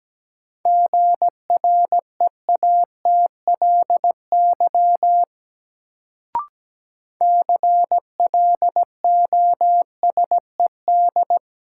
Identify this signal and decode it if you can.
Morse code, heard as a keyed tone, about 17 wpm, 700 Hz. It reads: GREATLY CLOSED
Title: Sentences from Top 1400 Words - New Word Drill Alt - 17wpm